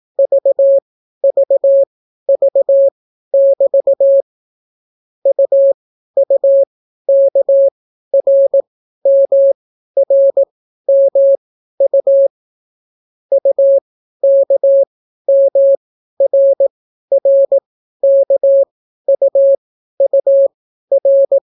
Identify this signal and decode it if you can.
Morse code, heard as a keyed tone, about 18 wpm, spaced as for 12 wpm, 550 Hz. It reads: VVV<BT> UUKRMRMU UKMRRKUUR